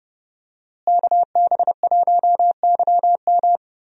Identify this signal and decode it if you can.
Morse code, heard as a keyed tone, about 31 wpm, 700 Hz. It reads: KB1YM
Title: USA Call Signs - 5 Characters - Repeated 2x - 30wpm